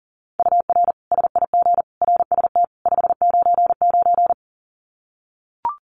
Morse code, heard as a keyed tone, about 40 wpm, 700 Hz. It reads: UR SIG RST 599
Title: Common QSO Phrases 40wpm